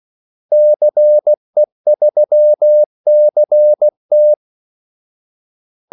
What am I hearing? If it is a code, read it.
Morse code, heard as a keyed tone, about 16 wpm, 600 Hz. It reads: CE3CT
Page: Kaldesignaler 16 wpm